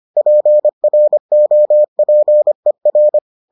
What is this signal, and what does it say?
Morse code, heard as a keyed tone, about 25 wpm, 600 Hz. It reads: PROPER